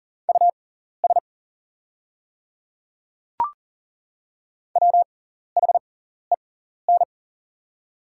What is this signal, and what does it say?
Morse code, heard as a keyed tone, about 40 wpm, spaced as for 14 wpm, 690 Hz. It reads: US WHEN
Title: 6x Character Spacing - Top 200 Words - 40wpm